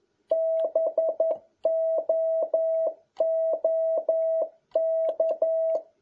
Morse code, heard as a keyed tone, about 11 wpm, 650 Hz. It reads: BOOK
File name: jh8cbh / 2021cw